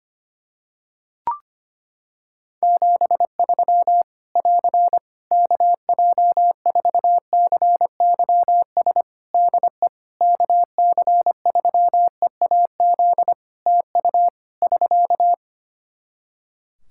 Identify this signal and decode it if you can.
Morse code, heard as a keyed tone, about 25 wpm, 700 Hz. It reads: 73 <AR> KJ4CYS DE KC3EAZ TU <SK>
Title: Practice QSO - Exchange 4 - Ending - 25wpm